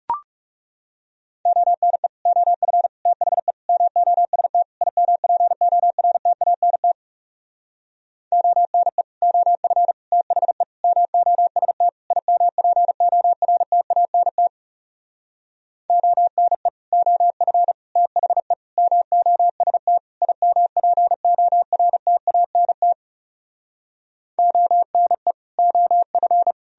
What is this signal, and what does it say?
Morse code, heard as a keyed tone, about 40 wpm, 700 Hz. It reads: ONE OF THE MOST IMPORTANT ONE OF THE MOST IMPORTANT ONE OF THE MOST IMPORTANT ONE OF